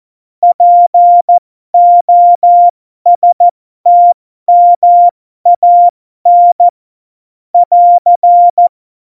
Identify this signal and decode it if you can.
Morse code, heard as a keyed tone, about 14 wpm, spaced as for 12 wpm, 700 Hz. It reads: POSTMAN <AR>
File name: Morsekurs / Witze